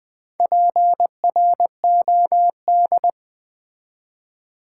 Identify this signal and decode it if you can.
Morse code, heard as a keyed tone, about 20 wpm, 700 Hz. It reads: PROD
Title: Build A Word - 4 Letters At A Time - 10-Letter Words - 20wpm